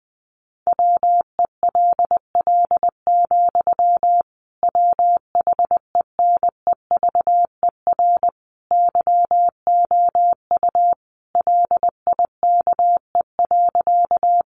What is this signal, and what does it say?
Morse code, heard as a keyed tone, about 19 wpm, 700 Hz. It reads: WELL, WHENEVER YOU LIKE.
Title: Sentences from Top 1200 Words 20wpm